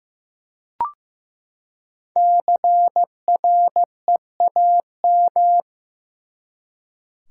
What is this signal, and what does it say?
Morse code, heard as a keyed tone, about 15 wpm, 700 Hz. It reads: CREAM
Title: Five Letter Words - 15wpm